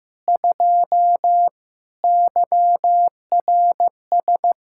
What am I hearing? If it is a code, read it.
Morse code, heard as a keyed tone, about 15 wpm, 700 Hz. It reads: 2 YRS
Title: Practice QSO - Exchange 3 - More Info - 15wpm